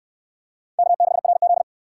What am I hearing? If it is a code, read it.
Morse code, heard as a keyed tone, about 57 wpm, 700 Hz. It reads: K6RZ